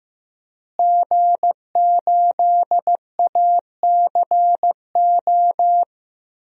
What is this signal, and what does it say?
Morse code, heard as a keyed tone, about 15 wpm, 700 Hz. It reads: G8ACO
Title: DXCC-100 Call Signs - 5 Characters - Repeated 3x - 15wpm